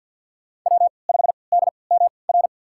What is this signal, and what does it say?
Morse code, heard as a keyed tone, about 47 wpm, spaced as for 28 wpm, 700 Hz. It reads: W5DMR
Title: USA Call Signs - 5 Characters - 3x Character Spacing - 50wpm